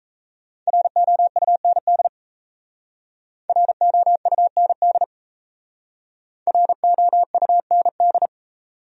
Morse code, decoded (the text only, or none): ROUND ROUND ROUND